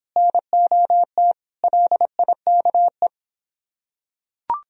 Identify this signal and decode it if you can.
Morse code, heard as a keyed tone, about 24 wpm, 700 Hz. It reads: NOT LIKE